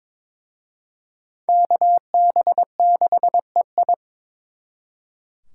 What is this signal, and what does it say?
Morse code, heard as a keyed tone, about 26 wpm, 700 Hz. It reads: KB6EI
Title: Speed Racing - USA 2x2 Call Signs 22wpm